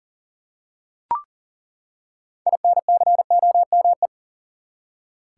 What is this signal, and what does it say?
Morse code, heard as a keyed tone, about 40 wpm, 700 Hz. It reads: INCOME
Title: Six Letter Words - 40wpm